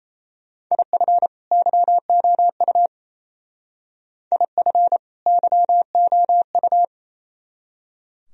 Morse code, heard as a keyed tone, about 36 wpm, 700 Hz. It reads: IF YOU IF YOU